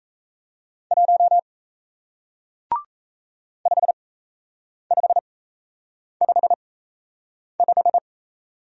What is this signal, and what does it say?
Morse code, heard as a keyed tone, about 38 wpm, spaced as for 8 wpm, 700 Hz. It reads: 1 5555